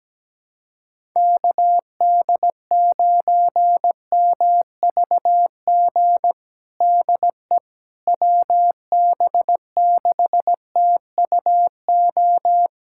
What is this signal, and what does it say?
Morse code, heard as a keyed tone, about 17 wpm, 700 Hz. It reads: KD9MVG DE WB6TUO
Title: USA Callsign DE Callsign 17wpm